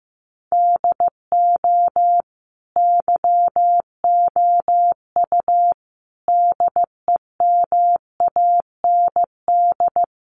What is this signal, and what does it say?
Morse code, heard as a keyed tone, about 15 wpm, 700 Hz. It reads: DO YOU DEMAND
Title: Sentences from Top 800 Words 15wpm